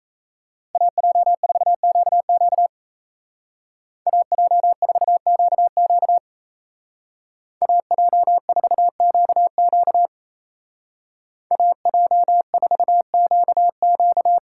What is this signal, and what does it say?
Morse code, heard as a keyed tone, about 34 wpm, 700 Hz. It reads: AJ4QQ AJ4QQ AJ4QQ AJ4QQ
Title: Speed Racing - USA 2x2 Call Signs 28wpm